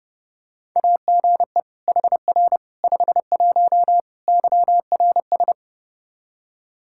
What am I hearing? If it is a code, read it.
Morse code, heard as a keyed tone, about 30 wpm, 700 Hz. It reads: AGE HR 51 YRS
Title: Practice QSO - Exchange 3 - More Info - 30wpm